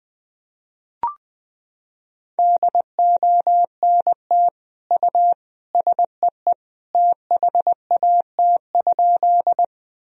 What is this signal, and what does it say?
Morse code, heard as a keyed tone, about 20 wpm, 700 Hz. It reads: DONT U SEE THAT?
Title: Sentences from Top 300 Words and Abbreviations 20wpm